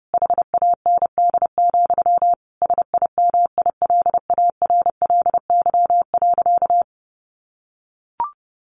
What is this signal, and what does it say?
Morse code, heard as a keyed tone, about 30 wpm, 700 Hz. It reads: HAND, SIMILARLY.
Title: Sentences from Top 1400 Words 30wpm